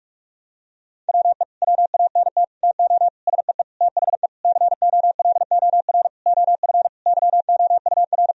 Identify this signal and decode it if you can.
Morse code, heard as a keyed tone, about 41 wpm, 700 Hz. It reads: WE WANT TO SEE THE COLOR OF YOUR
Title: Sentences from Top 1000 Words - Review Part 2 - 45wpm